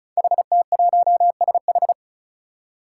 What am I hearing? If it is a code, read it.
Morse code, heard as a keyed tone, about 36 wpm, 700 Hz. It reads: HT1SH
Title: Five Challenging Letter-Number Combinations 35wpm